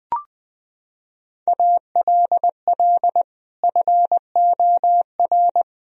Morse code, heard as a keyed tone, about 20 wpm, 700 Hz. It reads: ALL FOR
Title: Sentences from Top 1000 Words - Review Part 2 - 20wpm